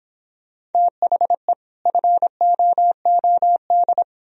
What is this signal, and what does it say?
Morse code, heard as a keyed tone, about 31 wpm, 700 Hz. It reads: THE FOOD